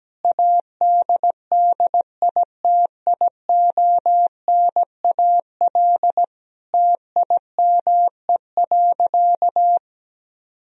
Morse code, heard as a keyed tone, about 17 wpm, 700 Hz. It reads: ADDITIONAL TIME.
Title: Sentences from Top 1000 Words - Review Part 4 - 17wpm